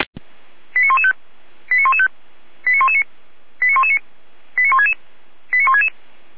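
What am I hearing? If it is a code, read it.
Five-tone selective call (ZVEI), heard as quick runs of five tones, tones of about 80 ms. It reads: 89115 89115 89119 89119 89166 89166
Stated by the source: ZVEI five-tone call received via SDR, archived by Feuerwehr Orlamünde (fire brigade).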